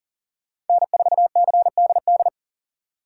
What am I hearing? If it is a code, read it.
Morse code, heard as a keyed tone, about 48 wpm, 700 Hz. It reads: N4CDD
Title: Speed Racing - USA 1x3 Call Signs 40wpm